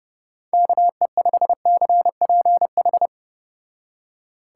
Morse code, heard as a keyed tone, about 31 wpm, 700 Hz. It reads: KE5CPH